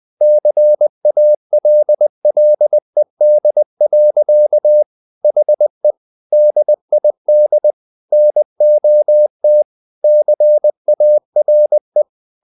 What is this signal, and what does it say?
Morse code, heard as a keyed tone, about 20 wpm, 600 Hz. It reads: CALLED. HE DID NOT CARE
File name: the old man and the sea part 2 20wpm